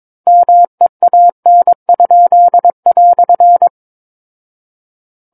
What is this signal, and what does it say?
Morse code, heard as a keyed tone, about 22 wpm, 700 Hz. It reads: MEAN?"
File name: Book One / 13-Chapter 13 - 22 WPM